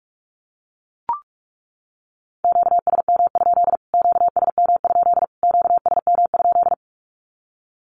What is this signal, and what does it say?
Morse code, heard as a keyed tone, about 44 wpm, 700 Hz. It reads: QSM? QSM? QSM?
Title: Q-Codes Repeated 3x 45wpm